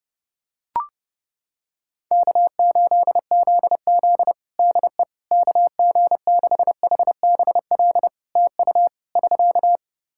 Morse code, heard as a keyed tone, about 30 wpm, 700 Hz. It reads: K8ZZ DE KG6HBL TU <SK>
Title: Practice QSO - Exchange 4 - Ending - 30wpm